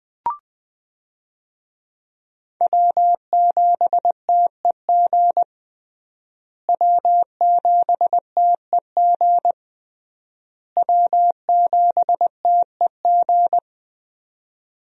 Morse code, heard as a keyed tone, about 20 wpm, 700 Hz. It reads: W7TEG W7TEG W7TEG
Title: USA Call Signs - 5 Characters - Repeated 3x - 20wpm